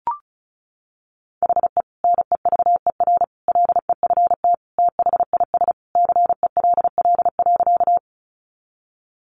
Morse code, heard as a keyed tone, about 34 wpm, 700 Hz. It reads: HE NEVER LEFT THIS CELL.
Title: Sentences from Top 1100 Words 35wpm